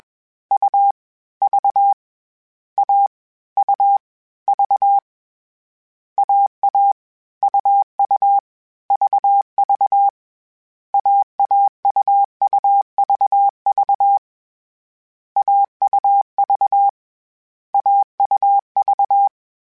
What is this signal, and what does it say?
Morse code, heard as a keyed tone, about 21 wpm, 800 Hz. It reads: U V A U V AA UU VV AAUUVV AUV AUV
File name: Difficult Combos / Difficult Combos-mp3s / AUV AUV4 @20